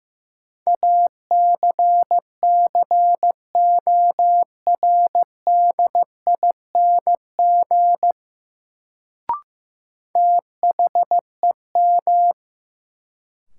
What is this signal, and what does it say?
Morse code, heard as a keyed tone, about 15 wpm, 700 Hz. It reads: ACCORDING THEM
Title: Sets of 2 Words - New Word Drill - 15wpm